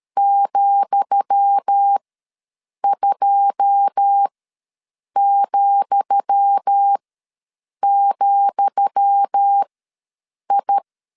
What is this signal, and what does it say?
Morse code, heard as a keyed tone, about 13 wpm, spaced as for 7 wpm, 800 Hz. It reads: ,2,,I
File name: LEZIONI / 10 SIGNS / 10D COMA ws 12wpm 3618